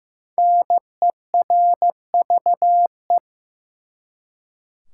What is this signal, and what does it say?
Morse code, heard as a keyed tone, about 15 wpm, 700 Hz. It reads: NERVE
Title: Five Letter Words - 15wpm